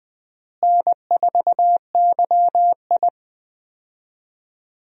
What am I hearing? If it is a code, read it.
Morse code, heard as a keyed tone, about 20 wpm, 700 Hz. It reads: N4YI
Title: USA Call Signs - 4 Characters - Repeated 2x - 20wpm